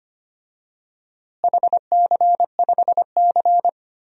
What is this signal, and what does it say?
Morse code, heard as a keyed tone, about 25 wpm, 700 Hz. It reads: HC5C
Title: DXCC-100 Call Signs - 4 Characters - Repeated 3x - 25wpm